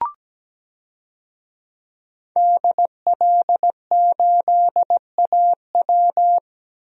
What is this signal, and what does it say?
Morse code, heard as a keyed tone, about 17 wpm, 700 Hz. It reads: DL8AW
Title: DXCC-100 Call Signs - 5 Characters - Repeated 2x - 17wpm